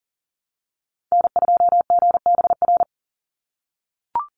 Morse code, heard as a keyed tone, about 40 wpm, 700 Hz. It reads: N2GBR
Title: NA SOTA Activator Callsigns 40wpm